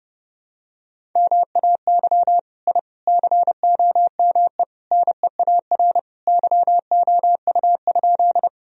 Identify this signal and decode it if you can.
Morse code, heard as a keyed tone, about 29 wpm, 700 Hz. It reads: MAY I COME NEAR YOU?
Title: Sentences from Top 1000 Words - Review Part 1 - 30wpm